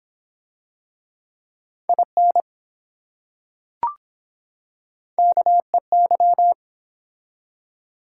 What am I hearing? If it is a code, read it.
Morse code, heard as a keyed tone, about 24 wpm, 700 Hz. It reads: IN KEY